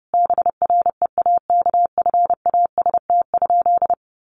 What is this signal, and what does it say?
Morse code, heard as a keyed tone, about 30 wpm, 700 Hz. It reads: BREAKFAST?
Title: Sentences from Top 1500 Words 30wpm